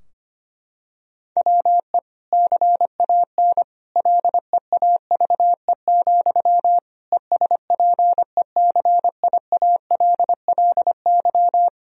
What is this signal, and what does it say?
Morse code, heard as a keyed tone, about 25 wpm, 700 Hz. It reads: WE CAN LEAVE, ESPECIALLY
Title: Sentences from Top 1000 Words - Review Part 4 - 25wpm